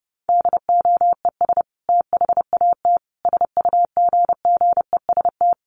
Morse code, heard as a keyed tone, about 30 wpm, 700 Hz. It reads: DOES THAT SUGGEST
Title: Sentences from Top 1300 Words 30wpm